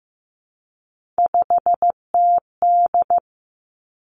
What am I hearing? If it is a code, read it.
Morse code, heard as a keyed tone, about 15 wpm, 700 Hz. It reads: 5TD
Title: Three Challenging Letter-Number Combinations 15wpm